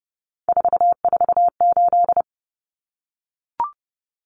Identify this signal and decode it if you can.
Morse code, heard as a keyed tone, about 30 wpm, 700 Hz. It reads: 448
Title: Numbers - 3-Digit 30wpm